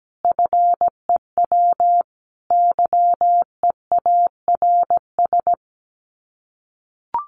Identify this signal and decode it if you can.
Morse code, heard as a keyed tone, about 17 wpm, 700 Hz. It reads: FEW YEARS
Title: Sets of 5 Words 17wpm